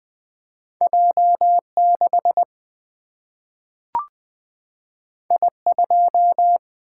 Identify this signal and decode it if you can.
Morse code, heard as a keyed tone, about 20 wpm, 700 Hz. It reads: J6 I2
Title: Single-Letter Single-Number 20wpm